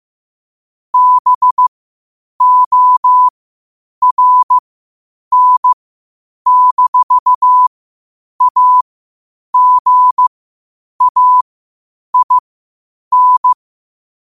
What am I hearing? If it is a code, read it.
Morse code, heard as a keyed tone, about 15 wpm, spaced as for 8 wpm, 1000 Hz. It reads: BORN-AGAIN
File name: qotdfiles / 8wpm